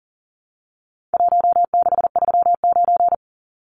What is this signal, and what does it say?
Morse code, heard as a keyed tone, about 40 wpm, 700 Hz. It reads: 1639
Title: Numbers - Contest Style - 4-Digit 40wpm